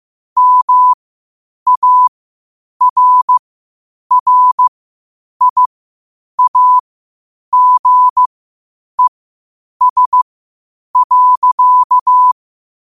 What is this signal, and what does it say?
Morse code, heard as a keyed tone, about 15 wpm, spaced as for 8 wpm, 1000 Hz. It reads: MARRIAGES.
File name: 8wpm